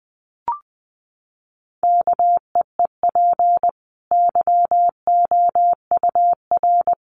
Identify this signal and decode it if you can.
Morse code, heard as a keyed tone, about 20 wpm, 700 Hz. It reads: KEEP YOUR
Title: Sentences from Top 1200 Words 20wpm